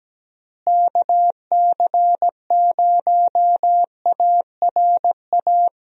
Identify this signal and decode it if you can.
Morse code, heard as a keyed tone, about 17 wpm, 700 Hz. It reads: KC0ARA
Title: USA Call Signs - 6 Characters - Repeated 2x - 17wpm